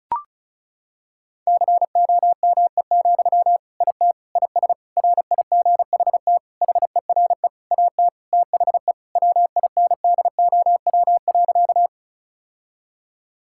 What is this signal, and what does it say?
Morse code, heard as a keyed tone, about 36 wpm, 700 Hz. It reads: COME, IT IS RIGHT HERE AT THE WINDOW.
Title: Sentences from Top 1600 Words 35wpm